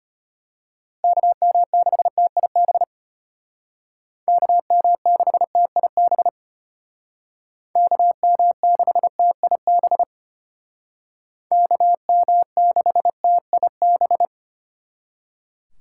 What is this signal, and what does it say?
Morse code, heard as a keyed tone, about 32 wpm, 700 Hz. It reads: KM6TIB KM6TIB KM6TIB KM6TIB